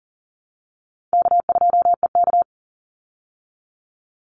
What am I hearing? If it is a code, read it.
Morse code, heard as a keyed tone, about 40 wpm, 700 Hz. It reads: K2EK
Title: Easy Call Signs Repeated 2x - 40wpm